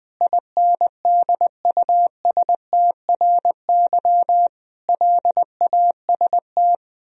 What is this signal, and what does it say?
Morse code, heard as a keyed tone, about 20 wpm, 700 Hz. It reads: INDUSTRY LAST
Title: Sentences from Top 1000 Words - Review Part 4 - 20wpm